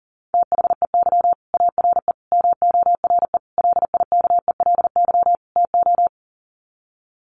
Morse code, heard as a keyed tone, about 40 wpm, 700 Hz. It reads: THEY ARE MORE LIKELY TO
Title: Sets of 5 Words Encore2 40wpm